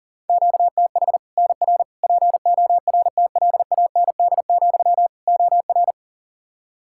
Morse code, heard as a keyed tone, about 40 wpm, 700 Hz. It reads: QTH NR PORTLAND, OR
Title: Practice QSO - Exchange 1 - Beginning - 40wpm